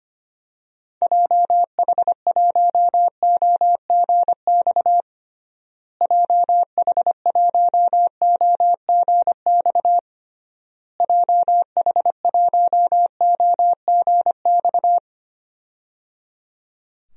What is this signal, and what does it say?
Morse code, heard as a keyed tone, about 25 wpm, 700 Hz. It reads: JH1OGX JH1OGX JH1OGX
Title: DXCC-100 Call Signs - 6 Characters - Repeated 3x - 25wpm